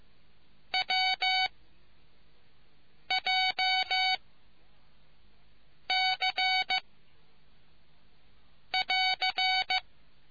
Morse code, heard as a keyed tone, about 15 wpm, 750 Hz. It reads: W J C <AR>